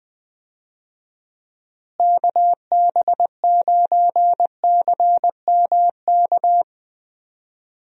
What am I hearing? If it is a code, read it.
Morse code, heard as a keyed tone, about 20 wpm, 700 Hz. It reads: KB9CMK
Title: USA Call Signs - 6 Characters - Repeated 2x - 20wpm